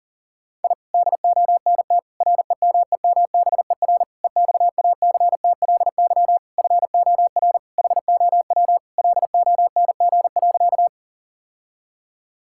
Morse code, heard as a keyed tone, about 40 wpm, 700 Hz. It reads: I DONT REMEMBER EXACTLY FOR HOW LONG.
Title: Sentences from Top 1000 Words - Review Part 6 - 40wpm